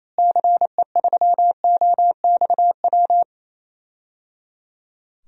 Morse code, heard as a keyed tone, about 28 wpm, 700 Hz. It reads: CE3OXW
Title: DXCC-100 Call Signs - 6 Characters - Repeated 3x - 28wpm